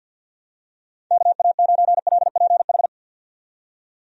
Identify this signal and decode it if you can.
Morse code, heard as a keyed tone, about 49 wpm, 700 Hz. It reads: KA9LPH
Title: USA Call Signs - 6 Characters - 50wpm